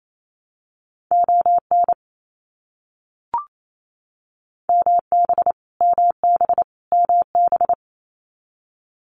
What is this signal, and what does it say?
Morse code, heard as a keyed tone, about 28 wpm, 700 Hz. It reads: ON MB MB MB
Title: Canadian Provinces and Territories - Repeated 3x - 28wpm